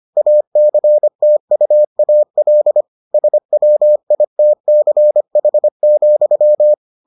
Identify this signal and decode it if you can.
Morse code, heard as a keyed tone, about 25 wpm, 600 Hz. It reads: ACTUAL SWITCH,